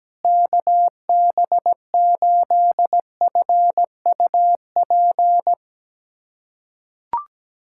Morse code, heard as a keyed tone, about 17 wpm, 700 Hz. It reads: KB8FUP
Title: USA Callsign DE Callsign 17wpm